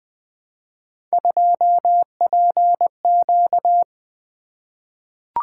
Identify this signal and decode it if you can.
Morse code, heard as a keyed tone, about 20 wpm, 700 Hz. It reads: 2PQ
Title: Three Letter-Number Combinations - 20wpm